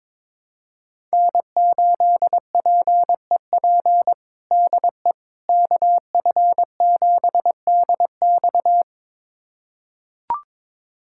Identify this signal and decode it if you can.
Morse code, heard as a keyed tone, about 22 wpm, 700 Hz. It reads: N8PEP DE KF7DX
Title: USA Callsign DE Callsign - 5 Characters - 22wpm